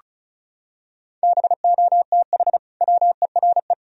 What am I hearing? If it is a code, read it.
Morse code, heard as a keyed tone, about 34 wpm, 700 Hz. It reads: BOTH WERE
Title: Sentences from Top 1000 Words - Review Part 2 - 35wpm